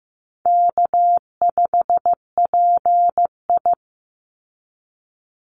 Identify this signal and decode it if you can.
Morse code, heard as a keyed tone, about 15 wpm, 700 Hz. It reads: K5PI
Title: Easy Call Signs Repeated 3x - 15wpm